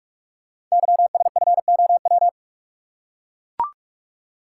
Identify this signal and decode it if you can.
Morse code, heard as a keyed tone, about 44 wpm, 690 Hz. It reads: YSFOW
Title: Code Groups - 5 Letter:Numbers - 45wpm